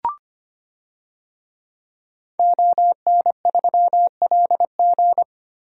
Morse code, heard as a keyed tone, about 25 wpm, 700 Hz. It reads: ON3LG
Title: DXCC-100 Call Signs - 5 Characters - Repeated 2x - 25wpm